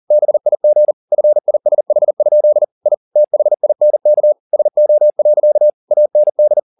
Morse code, heard as a keyed tone, about 40 wpm, 600 Hz. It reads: BIG FISH? I THINK SO. AND